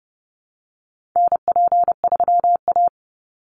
Morse code, heard as a keyed tone, about 30 wpm, 700 Hz. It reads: NP3A
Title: Call Signs Repeated 3x - 30wpm